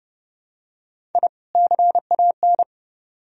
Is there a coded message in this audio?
Yes: Morse code from 1.1 to 2.6 s, heard as a keyed tone, about 29 wpm, 700 Hz. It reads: I CAN